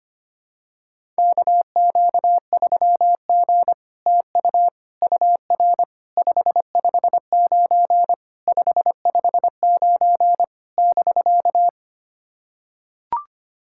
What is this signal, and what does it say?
Morse code, heard as a keyed tone, about 25 wpm, 700 Hz. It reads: KQ3G TU UR 559 559 <BK>
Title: SOTA QSO Practice 25wpm